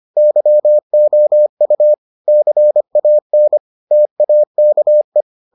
Morse code, heard as a keyed tone, about 25 wpm, 600 Hz. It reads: YOU CAN TAKE